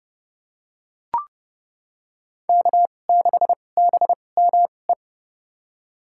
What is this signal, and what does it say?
Morse code, heard as a keyed tone, about 30 wpm, spaced as for 22 wpm, 700 Hz. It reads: K6BME
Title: USA Call Signs - 5 Characters - 2x Character Spacing - 30wpm